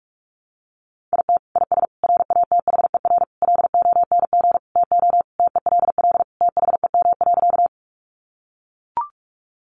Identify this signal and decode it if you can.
Morse code, heard as a keyed tone, about 44 wpm, 700 Hz. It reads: IT IS RATHER LONG TO TELL THEM.
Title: Sentences from Top 700 Words 45wpm